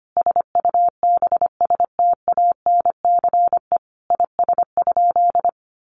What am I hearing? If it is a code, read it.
Morse code, heard as a keyed tone, about 25 wpm, 700 Hz. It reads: SUBSTANCE IS?
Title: Sentences from Top 1500 Words 25wpm